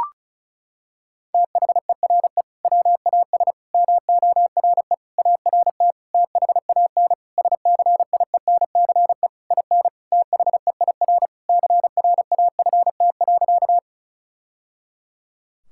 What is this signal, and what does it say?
Morse code, heard as a keyed tone, about 36 wpm, 700 Hz. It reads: THERE WAS MORE ART THAN SCIENCE IN THEIR CRAFT.
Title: Sentences from Top 1000 Words - Review Part 6 - 35wpm